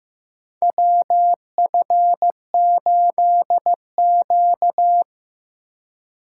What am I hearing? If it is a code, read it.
Morse code, heard as a keyed tone, about 15 wpm, 700 Hz. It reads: WF8Q
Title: USA Call Signs - 4 Characters - 2x Character Spacing - 15wpm